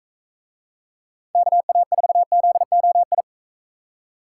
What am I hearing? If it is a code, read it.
Morse code, heard as a keyed tone, about 42 wpm, 700 Hz. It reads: KA4ZOI